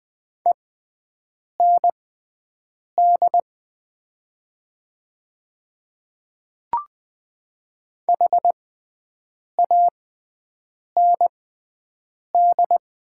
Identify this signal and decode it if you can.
Morse code, heard as a keyed tone, about 20 wpm, spaced as for 7 wpm, 700 Hz. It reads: END HAND
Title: 6x Character Spacing - Top 200 Words - 20wpm